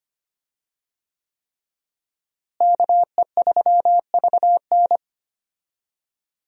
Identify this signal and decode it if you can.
Morse code, heard as a keyed tone, about 25 wpm, 700 Hz. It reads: KE3VN
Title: USA Call Signs - 5 Characters - Repeated 3x - 25wpm